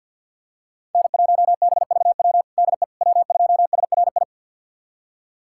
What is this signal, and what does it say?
Morse code, heard as a keyed tone, about 52 wpm, 700 Hz. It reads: N1BVW DE W2SRI
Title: USA Callsign DE Callsign - 5 Characters - 50wpm